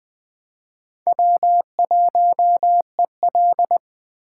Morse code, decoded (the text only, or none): W1EL